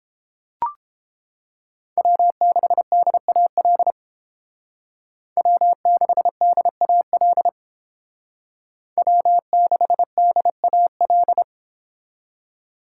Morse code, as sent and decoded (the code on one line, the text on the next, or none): .-- -.... -.. .- .-.. / .-- -.... -.. .- .-.. / .-- -.... -.. .- .-..
W6DAL W6DAL W6DAL